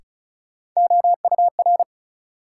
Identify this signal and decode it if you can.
Morse code, heard as a keyed tone, about 36 wpm, 700 Hz. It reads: OUR
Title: Sets of 2 Words - New Word Drill - 35wpm